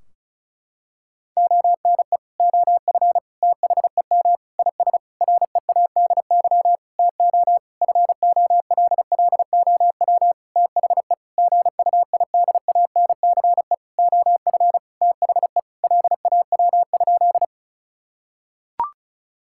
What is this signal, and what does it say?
Morse code, heard as a keyed tone, about 36 wpm, 700 Hz. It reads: ONE OF THEM IS READY TO FOLLOW THE GUIDANCE OF THE LAW?
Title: Sentences from Top 1600 Words 35wpm